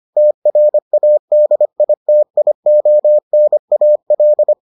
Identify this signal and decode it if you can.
Morse code, heard as a keyed tone, about 25 wpm, 600 Hz. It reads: TRADITIONAL